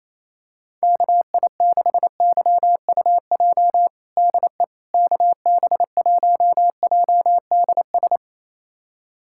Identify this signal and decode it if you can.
Morse code, heard as a keyed tone, about 28 wpm, 700 Hz. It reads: KI6YUJ DE KB1JDS
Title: USA Callsign DE Callsign 28wpm